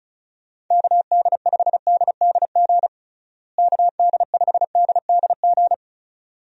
Morse code, heard as a keyed tone, about 34 wpm, 700 Hz. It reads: KD5DDG KD5DDG
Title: USA Call Signs - 6 Characters - Repeated 2x - 35wpm